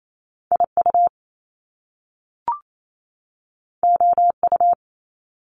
Morse code, heard as a keyed tone, about 28 wpm, 700 Hz. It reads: IU OU
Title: Top Two Letter Combinations in Words 28wpm